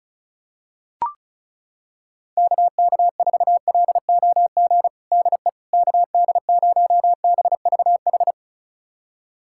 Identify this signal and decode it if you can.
Morse code, heard as a keyed tone, about 35 wpm, 700 Hz. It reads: KK4LOG DE KD0BVH
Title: USA Callsign DE Callsign - 6 Characters - 35wpm